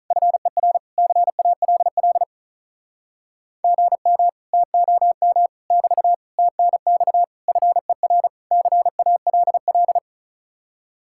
Morse code, heard as a keyed tone, about 44 wpm, 700 Hz. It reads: FER CALL GM TOM <BT> TNX FER CALL